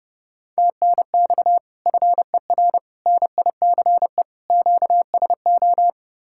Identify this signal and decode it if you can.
Morse code, heard as a keyed tone, about 30 wpm, 700 Hz. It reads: TNX FER NICE QSO